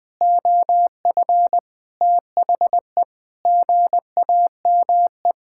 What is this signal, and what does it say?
Morse code, heard as a keyed tone, about 20 wpm, 700 Hz. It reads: OF THE GAME
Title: Sentences with 6-Word Phrases - 20wpm